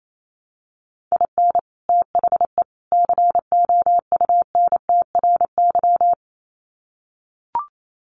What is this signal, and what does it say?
Morse code, heard as a keyed tone, about 28 wpm, 700 Hz. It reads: IN THE COUNTRY
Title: Sets of 3 Words 28wpm